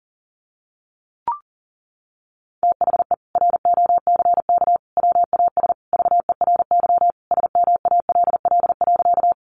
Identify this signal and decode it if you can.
Morse code, heard as a keyed tone, about 40 wpm, 700 Hz. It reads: THE ROCK WAS VERY SMALL.
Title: Sentences from Top 400 Words 40wpm